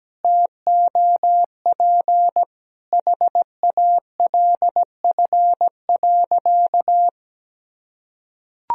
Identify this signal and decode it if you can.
Morse code, heard as a keyed tone, about 17 wpm, 700 Hz. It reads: TOP HALF.
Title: Sentences from Top 1000 Words - Review Part 2 - 17wpm